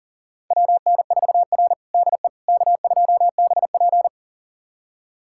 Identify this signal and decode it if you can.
Morse code, heard as a keyed tone, about 40 wpm, 700 Hz. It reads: WN4R DE K2BP
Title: USA Callsign DE Callsign - 4 Characters - 40wpm